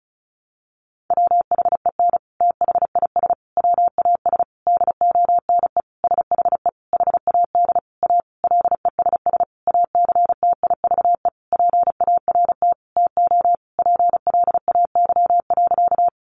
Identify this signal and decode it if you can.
Morse code, heard as a keyed tone, about 34 wpm, 700 Hz. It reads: WHEN THIS WAS DONE SHE HAD A LESS ACTIVE PART TO PLAY.
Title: Sentences from Top 900 Words 35wpm